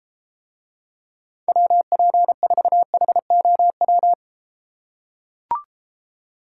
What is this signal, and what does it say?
Morse code, heard as a keyed tone, about 33 wpm, 700 Hz. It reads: WP4HOW